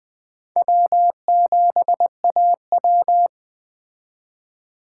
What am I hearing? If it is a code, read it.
Morse code, heard as a keyed tone, about 20 wpm, 700 Hz. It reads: W7AW